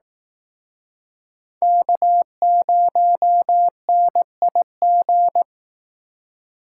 Morse code, heard as a keyed tone, about 19 wpm, 700 Hz. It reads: K0NIG